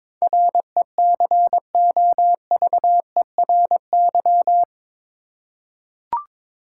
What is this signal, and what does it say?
Morse code, heard as a keyed tone, about 21 wpm, 700 Hz. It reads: RECOVERY
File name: practice / 22wpm / Eight Letter Words - 22wpm